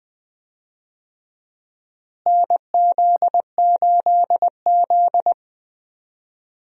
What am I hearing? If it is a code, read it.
Morse code, heard as a keyed tone, about 21 wpm, 700 Hz. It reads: NZ8Z
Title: USA Call Signs - 4 Characters - Repeated 2x - 20wpm